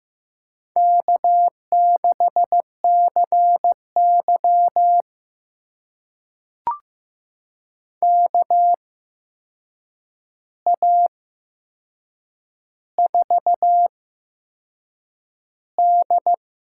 Morse code, heard as a keyed tone, about 15 wpm, spaced as for 6 wpm, 700 Hz. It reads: K6CY KA4D
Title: USA Call Signs - 4 Characters - 8x Character Spacing - 15wpm